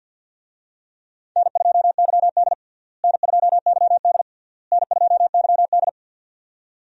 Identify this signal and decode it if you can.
Morse code, heard as a keyed tone, about 51 wpm, 700 Hz. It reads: N2YD N2YD N2YD
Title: USA Call Signs - 4 Characters - Repeated 3x - 50wpm